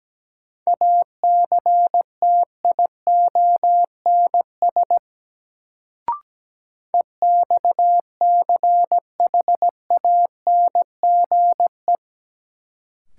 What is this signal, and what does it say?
Morse code, heard as a keyed tone, about 17 wpm, 700 Hz. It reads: ACTIONS EXCHANGE
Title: Sentences from Top 1000 Words - Review Part 6 - New Word Drill - 17wpm